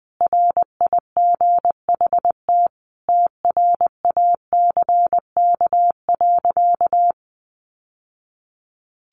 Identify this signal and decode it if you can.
Morse code, heard as a keyed tone, about 20 wpm, 700 Hz. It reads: RIGHT TRACK.
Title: Sentences from Top 1100 Words 20wpm